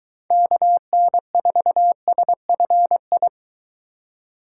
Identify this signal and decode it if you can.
Morse code, heard as a keyed tone, about 22 wpm, 700 Hz. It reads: KN4SFI